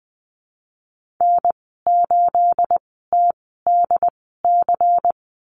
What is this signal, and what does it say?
Morse code, heard as a keyed tone, about 20 wpm, spaced as for 14 wpm, 700 Hz. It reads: N8TDC
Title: 245 Lesson 39g - Letter z - Call Signs 20wpm